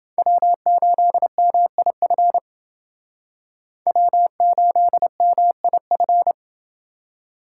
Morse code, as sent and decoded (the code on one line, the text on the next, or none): .-- ---.. -- .. ..-. / .-- ---.. -- .. ..-.
W8MIF W8MIF